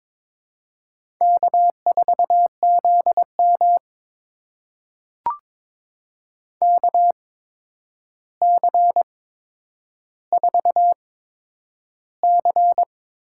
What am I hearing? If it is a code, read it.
Morse code, heard as a keyed tone, about 22 wpm, spaced as for 9 wpm, 700 Hz. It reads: K4ZM KC4C
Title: USA Call Signs - 8x Character Spacing - 22wpm